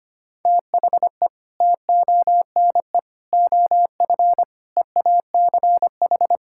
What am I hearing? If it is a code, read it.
Morse code, heard as a keyed tone, about 25 wpm, 700 Hz. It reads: THE TONE OF EACH
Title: Sentences from Top 1600 Words 25wpm